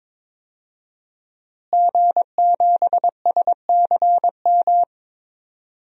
Morse code, heard as a keyed tone, about 22 wpm, 700 Hz. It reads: G7SCM
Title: DXCC-100 Call Signs - 5 Characters - Repeated 2x - 22wpm